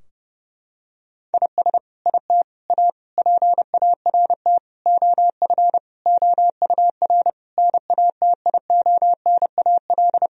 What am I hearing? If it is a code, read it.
Morse code, heard as a keyed tone, about 30 wpm, 700 Hz. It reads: IS IT A PART OF OUR NATIONAL
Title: Sentences from Top 1000 Words - Review Part 4 - 30wpm